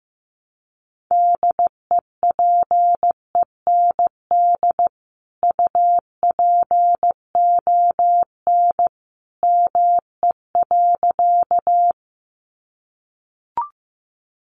Morse code, heard as a keyed tone, about 14 wpm, 700 Hz. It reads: DEPEND UPON ME.
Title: Sentences from Top 1300 Words 15wpm